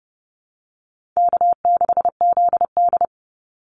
Morse code, heard as a keyed tone, about 30 wpm, 700 Hz. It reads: K6ZD
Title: Easy Call Signs Repeated 3x - 30wpm